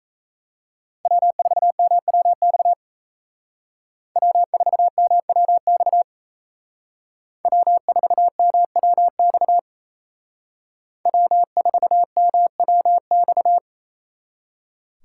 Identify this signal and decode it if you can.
Morse code, heard as a keyed tone, about 36 wpm, 700 Hz. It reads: W4MWX W4MWX W4MWX W4MWX